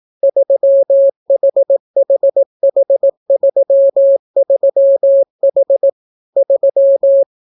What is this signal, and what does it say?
Morse code, heard as a keyed tone, about 18 wpm, 550 Hz. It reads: 3HHH33H 3